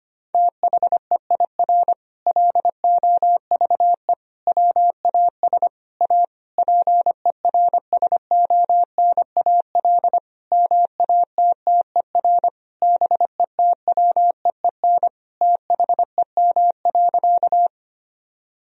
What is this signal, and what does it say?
Morse code, heard as a keyed tone, about 25 wpm, 700 Hz. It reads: THEIR LOVE WAS A PERSONAL MATTER BETWEEN THEM.
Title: Sentences from Top 1000 Words - Review Part 4 - 25wpm